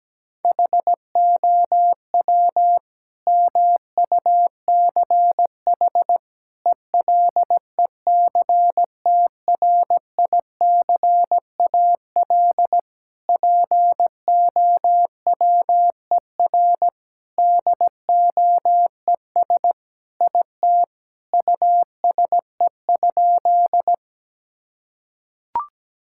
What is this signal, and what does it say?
Morse code, heard as a keyed tone, about 17 wpm, 700 Hz. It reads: HOW MUCH ELECTRICAL POWER DOES IT USE?
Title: Sentences from Top 1000 Words - Review Part 6 - 17wpm